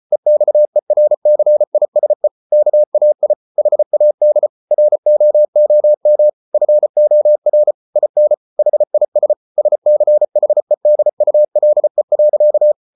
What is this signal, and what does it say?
Morse code, heard as a keyed tone, about 34 wpm, 600 Hz. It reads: EXERCISE KAI HAD ROOM FOR IN HIS SCHEDULE.